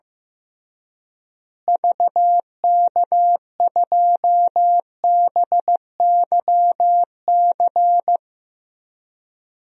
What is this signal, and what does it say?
Morse code, heard as a keyed tone, about 15 wpm, 700 Hz. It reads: VK2BYC
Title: DXCC-100 Call Signs - 6 Characters - Repeated 3x - 15wpm